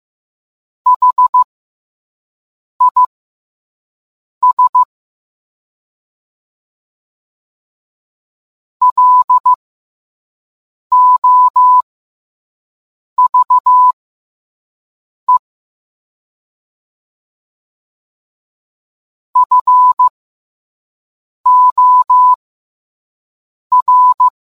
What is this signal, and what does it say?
Morse code, heard as a keyed tone, about 15 wpm, spaced as for 5 wpm, 1000 Hz. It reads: HIS LOVE FOR